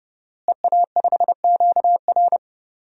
Morse code, heard as a keyed tone, about 30 wpm, 700 Hz. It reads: EA5QR
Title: DXCC-100 Call Signs - 5 Characters - Repeated 2x - 30wpm